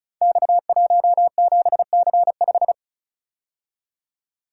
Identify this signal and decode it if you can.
Morse code, heard as a keyed tone, about 36 wpm, 700 Hz. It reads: X17C5
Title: Code Groups - 5 Letter:Numbers - 35wpm